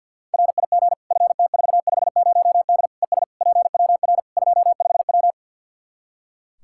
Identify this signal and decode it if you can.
Morse code, heard as a keyed tone, about 47 wpm, 700 Hz. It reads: RIG FT450D ES PWR 25W
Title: Practice QSO - Exchange 2 - Info - 50wpm